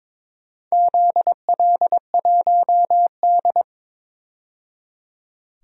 Morse code, heard as a keyed tone, about 22 wpm, 700 Hz. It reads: ZL1D
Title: DXCC-100 Call Signs - 4 Characters - 22wpm